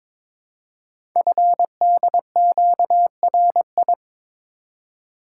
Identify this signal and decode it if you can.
Morse code, heard as a keyed tone, about 22 wpm, 700 Hz. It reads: FDQRI
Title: Code Groups - 5 Letters - 22wpm